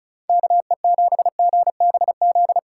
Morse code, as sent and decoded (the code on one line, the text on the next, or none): -.- . --... --. -... --..
KE7GBZ